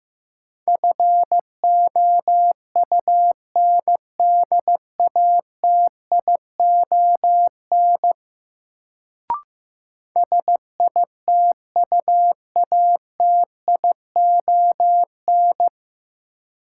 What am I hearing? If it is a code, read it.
Morse code, heard as a keyed tone, about 15 wpm, 700 Hz. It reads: FOUNDATION SITUATION
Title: Sentences from Top 1200 Words - New Word Drill - 15wpm